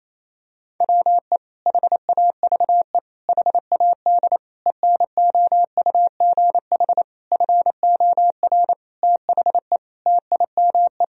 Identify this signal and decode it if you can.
Morse code, heard as a keyed tone, about 28 wpm, 700 Hz. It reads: WE HAVE HAD ENOUGH FOR THE TIME
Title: Sentences from Top 1000 Words - Review Part 3 - 28wpm